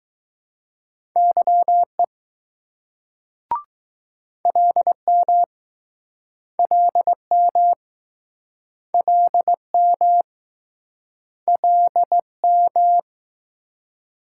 Morse code, heard as a keyed tone, about 21 wpm, 700 Hz. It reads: YE LM LM LM LM